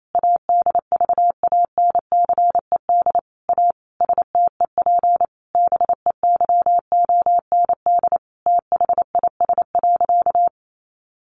Morse code, heard as a keyed tone, about 28 wpm, 700 Hz. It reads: ADVANCED A STEP BEYOND THIS.
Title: Sentences from Top 1200 Words 28wpm